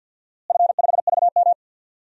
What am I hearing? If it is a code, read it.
Morse code, heard as a keyed tone, about 49 wpm, 700 Hz. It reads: F5VM
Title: DXCC-100 Call Signs - 4 Characters - Repeated 2x - 50wpm